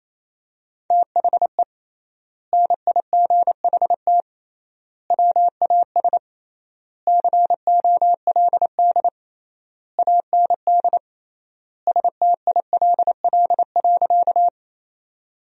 Morse code, heard as a keyed tone, about 28 wpm, 700 Hz. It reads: THE NIGHT WAS COLD AND STILL.
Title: 3x Word-Spacing - Sentences from Top 400 Words - 28wpm